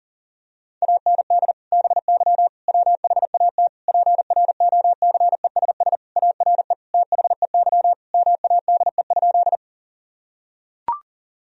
Morse code, heard as a keyed tone, about 40 wpm, 700 Hz. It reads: AND BY WHAT PROCESS ARE THEY MADE?
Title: Sentences from Top 1000 Words - Review Part 3 - 40wpm